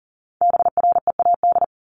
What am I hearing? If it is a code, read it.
Morse code, heard as a keyed tone, about 40 wpm, 700 Hz. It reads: BREAD
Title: Top 1501-1750 Words 40wpm